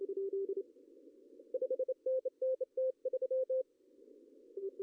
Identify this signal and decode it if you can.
Morse code, heard as a keyed tone, about 25 wpm, 520 Hz. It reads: 5NN T3
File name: CW / cqww-cw-k5tr